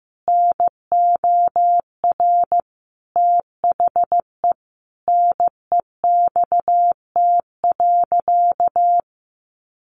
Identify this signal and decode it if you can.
Morse code, heard as a keyed tone, about 15 wpm, 700 Hz. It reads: NOR THE NEXT.
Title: Sentences from Top 1500 Words 15wpm